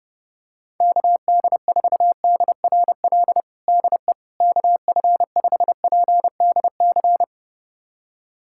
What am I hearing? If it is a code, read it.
Morse code, heard as a keyed tone, about 30 wpm, 700 Hz. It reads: KD4DRL DE KF5PDC
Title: USA Callsign DE Callsign - 6 Characters - 30wpm